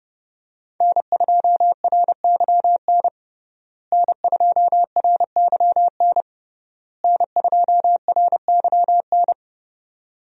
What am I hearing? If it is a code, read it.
Morse code, heard as a keyed tone, about 32 wpm, 700 Hz. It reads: N2RYN N2RYN N2RYN
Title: USA Call Signs - 5 Characters - Repeated 3x - 30wpm